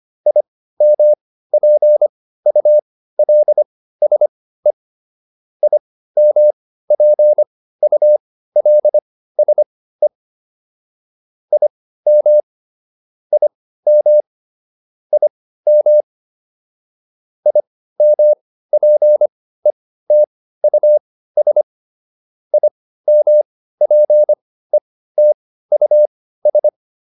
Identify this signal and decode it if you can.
Morse code, heard as a keyed tone, about 25 wpm, spaced as for 15 wpm, 600 Hz. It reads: IMPULSE IMPULSE IM IM IM IMPETUS IMPETUS